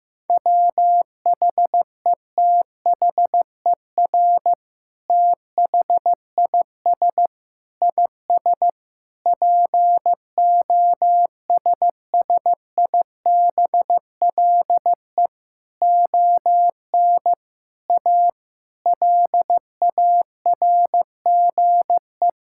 Morse code, heard as a keyed tone, about 16 wpm, 700 Hz. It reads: WHETHER THIS IS POSSIBLE ON A LARGE
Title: Sentences from Top 1000 Words - Review Part 5 - 15wpm